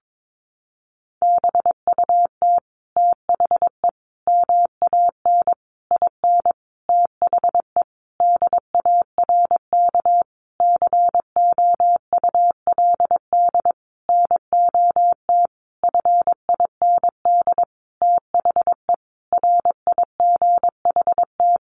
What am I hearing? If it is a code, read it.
Morse code, heard as a keyed tone, about 22 wpm, 700 Hz. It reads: BUT THE MAN IN THE DARK COULD NOT FIND THE RIGHT
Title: Sentences from Top 600 Words 22wpm